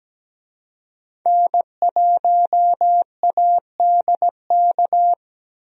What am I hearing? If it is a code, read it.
Morse code, heard as a keyed tone, about 17 wpm, 700 Hz. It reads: N1ADK
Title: USA Call Signs - 5 Characters - Repeated 3x - 17wpm